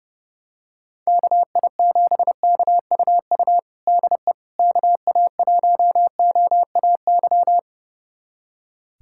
Morse code, heard as a keyed tone, about 30 wpm, 700 Hz. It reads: KI7KUU DE KA1OAY
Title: USA Callsign DE Callsign 30wpm